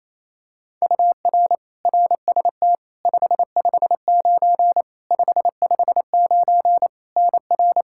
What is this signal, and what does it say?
Morse code, heard as a keyed tone, about 28 wpm, 700 Hz. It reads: UR RST 559 559 NR
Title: K1USN SST QSO Practice 28wpm